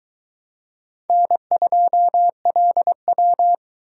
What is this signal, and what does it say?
Morse code, heard as a keyed tone, about 23 wpm, 700 Hz. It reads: N2LW